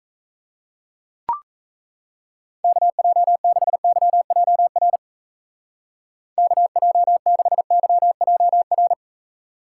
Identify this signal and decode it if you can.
Morse code, heard as a keyed tone, about 39 wpm, 700 Hz. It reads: KJ6YJR KJ6YJR